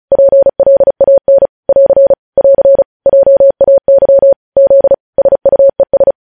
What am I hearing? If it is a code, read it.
Morse code, heard as a keyed tone, about 36 wpm, 550 Hz. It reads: PLAN <AR> <AR> JAY Z SUES
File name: podcast / News headlines-1x-35x35